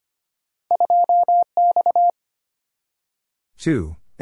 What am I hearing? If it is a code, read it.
Morse code, heard as a keyed tone, about 25 wpm, 690 Hz. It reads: 2X
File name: Two Letter-Number Combinations - 25wpm